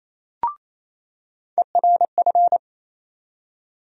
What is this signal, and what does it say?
Morse code, heard as a keyed tone, about 28 wpm, 700 Hz. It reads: ERF
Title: USA Call Sign Postfixes - 3 Letters 28wpm